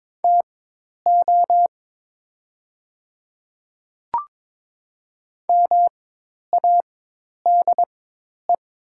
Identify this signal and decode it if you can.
Morse code, heard as a keyed tone, about 22 wpm, spaced as for 10 wpm, 700 Hz. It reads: TO MADE
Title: 4x Character Spacing - Top 200 Words - 22wpm